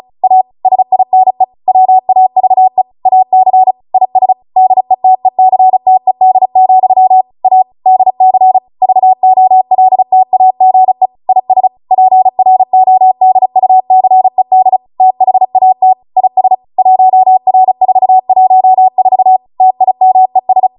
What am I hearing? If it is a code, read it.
Morse code, heard as a keyed tone, about 36 wpm, 750 Hz. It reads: A SINE WAVE AC IS DETECTED, A DC VOLTAGE IS PRODUCED THAT IS 1R414 TIMES